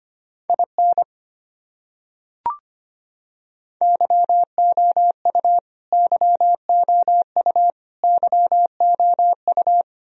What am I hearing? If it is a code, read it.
Morse code, heard as a keyed tone, about 25 wpm, 700 Hz. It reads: IN YOU YOU YOU
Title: Top 100 Words Repeated 25wpm